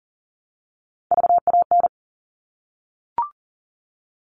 Morse code, heard as a keyed tone, about 40 wpm, 700 Hz. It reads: VAN
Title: All Cut Numbers - 3-Digit 40wpm